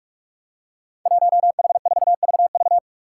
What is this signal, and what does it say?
Morse code, heard as a keyed tone, about 46 wpm, 700 Hz. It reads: 1H4VV
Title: Five Challenging Letter-Number Combinations 45wpm